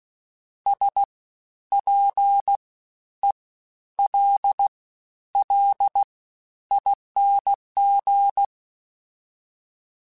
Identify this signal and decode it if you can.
Morse code, heard as a keyed tone, about 16 wpm, 800 Hz. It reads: S P E L L ING
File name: Special INGs-mp3s / Special INGs-mp3s / SlowToFaster ing @15